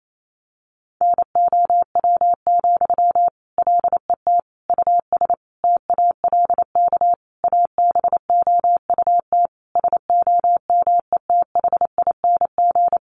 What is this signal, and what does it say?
Morse code, heard as a keyed tone, about 28 wpm, 700 Hz. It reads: NOW, LET US TALK ABOUT SOMETHING
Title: Sentences from Top 800 Words 28wpm